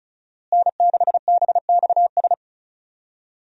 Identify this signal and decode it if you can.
Morse code, heard as a keyed tone, about 36 wpm, 700 Hz. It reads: N6BXS
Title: USA Call Signs - 5 Characters - Repeated 3x - 35wpm